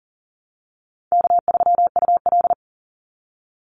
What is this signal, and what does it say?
Morse code, heard as a keyed tone, about 40 wpm, 700 Hz. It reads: K3UL
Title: Call Signs - 40wpm